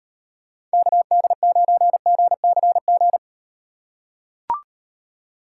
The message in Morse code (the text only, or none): KD9GCG